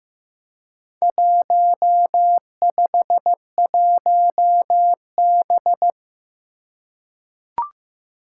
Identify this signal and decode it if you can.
Morse code, heard as a keyed tone, about 15 wpm, 700 Hz. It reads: 151B
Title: Four Challenging Letter-Number Combinations 15wpm